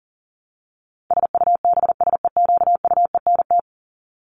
Morse code, heard as a keyed tone, about 40 wpm, 700 Hz. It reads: SUBSEQUENT
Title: LY Suffix - Top 100 Words with Base Words 40wpm